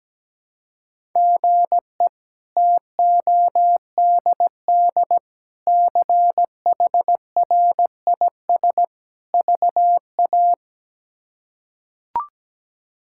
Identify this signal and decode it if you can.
Morse code, heard as a keyed tone, about 17 wpm, 700 Hz. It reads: GE TODD CHRIS VA
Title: K1USN SST QSO Practice - 17wpm